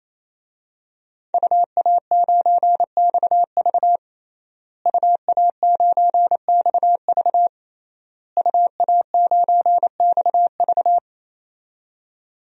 Morse code, heard as a keyed tone, about 28 wpm, 700 Hz. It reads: UA9XV UA9XV UA9XV
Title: DXCC-100 Call Signs - 5 Characters - Repeated 3x - 28wpm